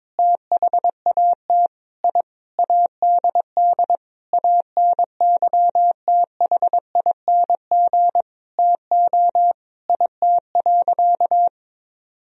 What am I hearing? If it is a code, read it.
Morse code, heard as a keyed tone, about 22 wpm, 700 Hz. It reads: THAT I ADD ANYTHING TO IT.
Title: Sentences from Top 1000 Words - Review Part 1 - 22wpm